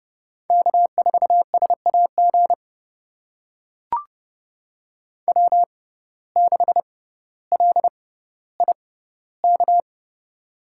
Morse code, heard as a keyed tone, about 30 wpm, spaced as for 15 wpm, 700 Hz. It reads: K4SAG W6LIK
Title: USA Call Signs - 5 Characters - 6x Character Spacing - 30wpm